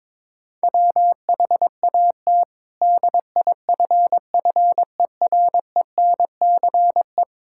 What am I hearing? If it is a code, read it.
Morse code, heard as a keyed tone, about 22 wpm, 700 Hz. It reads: WHAT DIFFERENCE
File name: practice / 22wpm / Sets of 8 Words - 22wpm